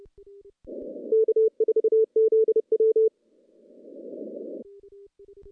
Morse code, heard as a keyed tone, about 30 wpm, 440 Hz. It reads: K4ZW
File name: contests / CW-SS-k5tr- -0600